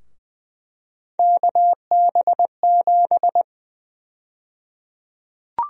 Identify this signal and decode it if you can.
Morse code, heard as a keyed tone, about 20 wpm, 700 Hz. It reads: KB7
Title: USA Call Sign Prefixes - Repeated 2x - 20wpm